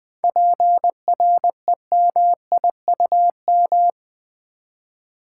Morse code, heard as a keyed tone, about 20 wpm, 700 Hz. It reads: PREMIUM
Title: 7 Letter Words - First 3 Letters Given - 20wpm